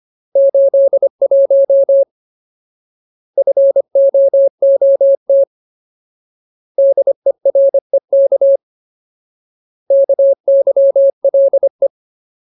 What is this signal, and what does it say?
Morse code, heard as a keyed tone, about 25 wpm, 550 Hz. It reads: 81 FOOT DEREK KYLE
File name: Ultimate Word List / 25WPM Ultimate